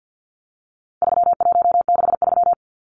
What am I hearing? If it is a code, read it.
Morse code, heard as a keyed tone, about 50 wpm, 700 Hz. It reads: 3163
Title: Numbers - 4-Digit 50wpm